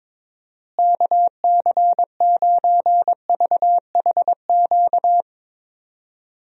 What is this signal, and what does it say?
Morse code, heard as a keyed tone, about 22 wpm, 700 Hz. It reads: KC9VHQ
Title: USA Call Signs - 6 Characters - Repeated 2x - 22wpm